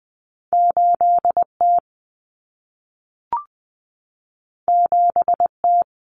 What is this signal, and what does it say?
Morse code, heard as a keyed tone, about 20 wpm, 700 Hz. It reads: 8T 7T
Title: Common Cut Numbers - 2-Digit 20wpm